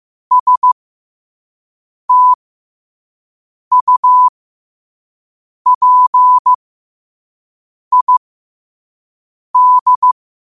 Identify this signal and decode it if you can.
Morse code, heard as a keyed tone, about 15 wpm, spaced as for 5 wpm, 1000 Hz. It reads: STUPID
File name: qotdfiles / 5wpm